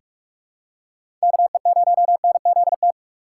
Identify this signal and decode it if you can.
Morse code, heard as a keyed tone, about 46 wpm, 700 Hz. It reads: KE0NZT
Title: USA Call Signs - 6 Characters - 3x Character Spacing - 45wpm